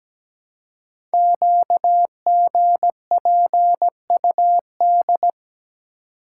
Morse code, heard as a keyed tone, about 17 wpm, 700 Hz. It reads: QGPUD
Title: Code Groups - 5 Letters - 17wpm